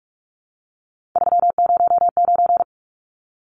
Morse code, heard as a keyed tone, about 44 wpm, 700 Hz. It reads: 309
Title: Numbers - 3-Digit 45wpm